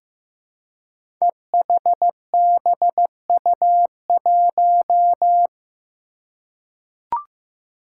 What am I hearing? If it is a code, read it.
Morse code, heard as a keyed tone, about 15 wpm, 700 Hz. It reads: EHBU1
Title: Five Challenging Letter-Number Combinations 15wpm